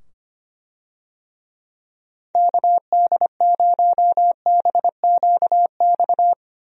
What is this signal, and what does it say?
Morse code, heard as a keyed tone, about 25 wpm, 700 Hz. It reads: KD0BQX
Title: USA Call Signs - 6 Characters - 25wpm